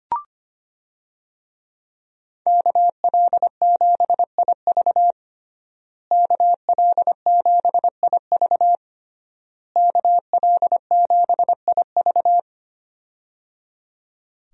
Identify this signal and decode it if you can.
Morse code, heard as a keyed tone, about 25 wpm, 700 Hz. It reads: KL7IV KL7IV KL7IV
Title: USA Call Signs - 5 Characters - Repeated 3x - 25wpm